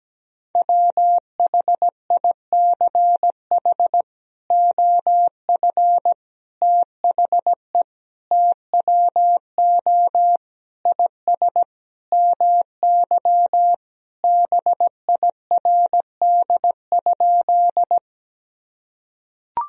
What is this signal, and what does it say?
Morse code, heard as a keyed tone, about 17 wpm, 700 Hz. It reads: WHICH OF THE TWO IS MY BIRD?
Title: Sentences from Top 1000 Words - Review Part 2 - 17wpm